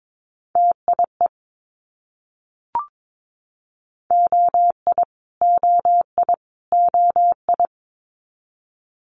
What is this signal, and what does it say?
Morse code, heard as a keyed tone, about 22 wpm, 700 Hz. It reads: TIE OI OI OI
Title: Multi-letter Phonemes Repeated 3x 22wpm